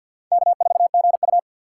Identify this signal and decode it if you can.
Morse code, heard as a keyed tone, about 47 wpm, 700 Hz. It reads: K4GU
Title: USA Call Signs - 4x Character Spacing - 50wpm